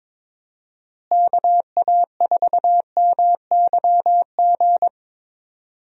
Morse code, heard as a keyed tone, about 23 wpm, 700 Hz. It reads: KA4MYG
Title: USA Call Signs - 6 Characters - Repeated 3x - 22wpm